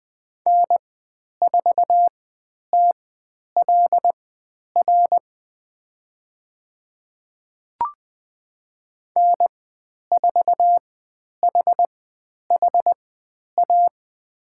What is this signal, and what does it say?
Morse code, heard as a keyed tone, about 20 wpm, spaced as for 10 wpm, 700 Hz. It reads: N4TLR N4HHA
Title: 109 Lesson 18g - Letter m - Call Signs 20wpm